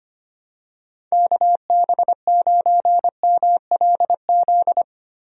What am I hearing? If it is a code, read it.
Morse code, heard as a keyed tone, about 25 wpm, 700 Hz. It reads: KB9MLZ
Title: USA Call Signs - 6 Characters - 25wpm